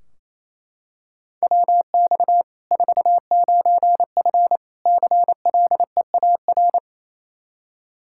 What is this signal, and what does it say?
Morse code, heard as a keyed tone, about 28 wpm, 700 Hz. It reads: WX 49F CLEAR
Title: Practice QSO - Exchange 2 - Info - 28wpm